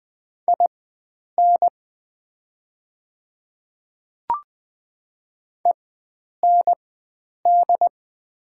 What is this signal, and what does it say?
Morse code, heard as a keyed tone, about 20 wpm, spaced as for 9 wpm, 700 Hz. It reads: IN END
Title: 4x Character Spacing - Top 500 Words - 20wpm